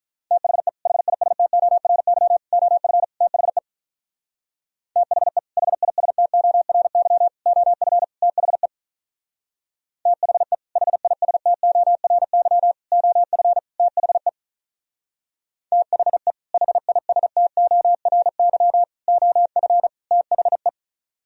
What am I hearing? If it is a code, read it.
Morse code, heard as a keyed tone, about 42 wpm, 700 Hz. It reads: THE HISTORY OF THE THE HISTORY OF THE THE HISTORY OF THE THE HISTORY OF THE
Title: Speed-Racing - Sets of 4 Words 35wpm